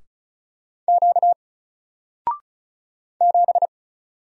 Morse code, heard as a keyed tone, about 36 wpm, 700 Hz. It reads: Q 7
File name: Single-Letter-Number-35wpm